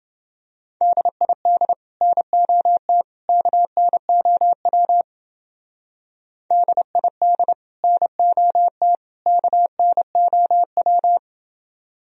Morse code, heard as a keyed tone, about 28 wpm, 700 Hz. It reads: DID NOT KNOW DID NOT KNOW